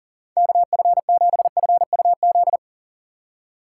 Morse code, heard as a keyed tone, about 40 wpm, 700 Hz. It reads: KF7FUZ
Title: USA Call Signs - 6 Characters - 2x Character Spacing - 40wpm